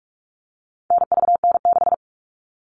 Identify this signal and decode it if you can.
Morse code, heard as a keyed tone, about 44 wpm, 700 Hz. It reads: NVN6
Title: All Cut Numbers - 4-Digit 45wpm